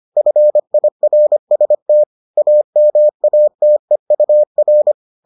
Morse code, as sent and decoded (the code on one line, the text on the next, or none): ..-. .. .-. ... - / .- -- .- - . ..- .-.
FIRST AMATEUR